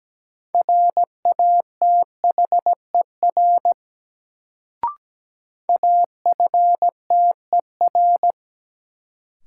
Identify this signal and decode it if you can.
Morse code, heard as a keyed tone, about 17 wpm, 700 Hz. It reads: RATHER AFTER
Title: Sets of 2 Words - New Word Drill - 17wpm